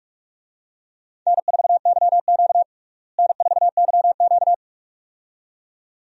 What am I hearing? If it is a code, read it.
Morse code, heard as a keyed tone, about 46 wpm, 700 Hz. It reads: N4YQ N4YQ
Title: USA Call Signs - 4 Characters - Repeated 2x - 45wpm